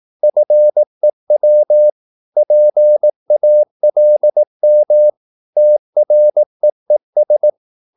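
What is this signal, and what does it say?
Morse code, heard as a keyed tone, about 18 wpm, 600 Hz. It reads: FEW PALM TREES
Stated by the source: Morse practice at 18 wpm, English text, from CW med Gnister.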